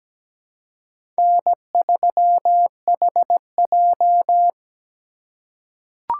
Morse code, heard as a keyed tone, about 17 wpm, 700 Hz. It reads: N3HJ
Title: USA Call Signs - 4 Characters - 3x Character Spacing - 17wpm